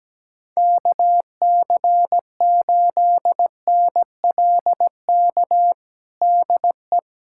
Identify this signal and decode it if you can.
Morse code, heard as a keyed tone, about 17 wpm, 700 Hz. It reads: KC8NLK DE
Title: USA Callsign DE Callsign 17wpm